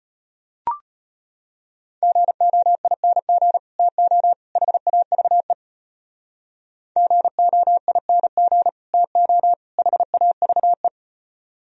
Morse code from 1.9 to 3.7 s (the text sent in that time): GOING